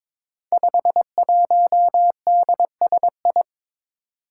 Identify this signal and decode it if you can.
Morse code, heard as a keyed tone, about 23 wpm, 700 Hz. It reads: 51DSI
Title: Five Challenging Letter-Number Combinations 22wpm